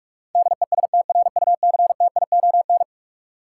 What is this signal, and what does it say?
Morse code, heard as a keyed tone, about 46 wpm, 700 Hz. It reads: DESTRUCTION
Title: Sets of 5 Words Encore - New Word Drill - 45wpm